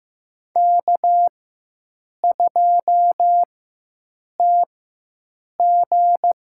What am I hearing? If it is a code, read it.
Morse code, heard as a keyed tone, about 15 wpm, spaced as for 7 wpm, 700 Hz. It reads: K2TG
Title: USA Call Signs - 4 Characters - 4x Character Spacing - 15wpm